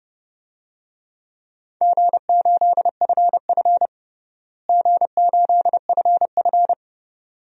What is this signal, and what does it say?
Morse code, heard as a keyed tone, about 32 wpm, 700 Hz. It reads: G8FF G8FF
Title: DXCC-100 Call Signs - 4 Characters - Repeated 2x - 30wpm